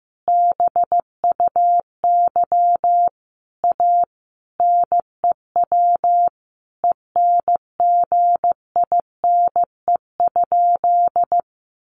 Morse code, heard as a keyed tone, about 15 wpm, 700 Hz. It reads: BUY A NEW ENGINE?
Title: Sentences from Top 1500 Words 15wpm